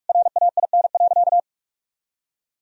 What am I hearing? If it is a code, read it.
Morse code, heard as a keyed tone, about 41 wpm, 700 Hz. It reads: RAIN.
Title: Sentences from Top 1000 Words - Review Part 1 - 45wpm